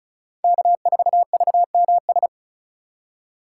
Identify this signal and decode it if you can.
Morse code, heard as a keyed tone, about 36 wpm, 700 Hz. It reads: K4VMS
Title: USA Call Signs - 4x Character Spacing - 35wpm